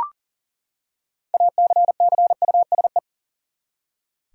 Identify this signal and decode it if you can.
Morse code, heard as a keyed tone, about 40 wpm, 700 Hz. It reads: ACCUSE
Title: Six Letter Words - 40wpm